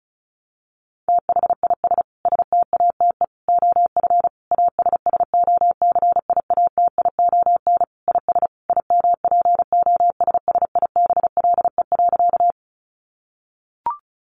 Morse code, heard as a keyed tone, about 34 wpm, 700 Hz. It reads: THIS STATE OF ASSOCIATION IS IMPOSSIBLE.
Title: Sentences from Top 1100 Words 35wpm